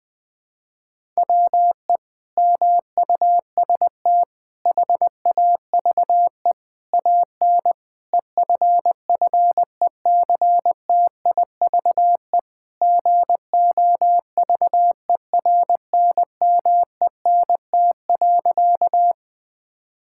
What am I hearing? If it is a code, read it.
Morse code, heard as a keyed tone, about 20 wpm, 700 Hz. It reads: WE MUST HAVE AN EFFECTIVE GOVERNMENT.
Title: Sentences from Top 1000 Words - Review Part 6 - 20wpm